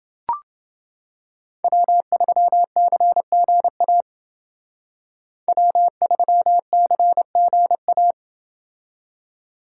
Morse code, heard as a keyed tone, about 29 wpm, 700 Hz. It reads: W3CGA W3CGA